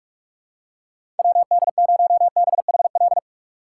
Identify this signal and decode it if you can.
Morse code, heard as a keyed tone, about 46 wpm, 700 Hz. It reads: WD0BHL
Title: USA Call Signs - 6 Characters - 6x Character Spacing - 45wpm